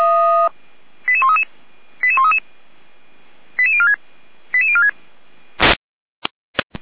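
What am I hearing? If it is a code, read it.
Five-tone selective call (ZVEI), heard as quick runs of five tones, tones of about 70 ms. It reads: 88133 88133 88046 88046
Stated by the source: ZVEI five-tone selective call received on SDR.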